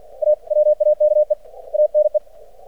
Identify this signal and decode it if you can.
Morse code, heard as a keyed tone, about 50 wpm, 600 Hz. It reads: AWAKE ANE
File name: archived / CRSnet